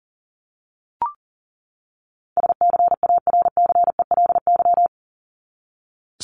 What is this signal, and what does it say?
Morse code, heard as a keyed tone, about 40 wpm, 700 Hz. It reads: SCARCELY
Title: LY Suffix - Top 100 Words with Base Words 40wpm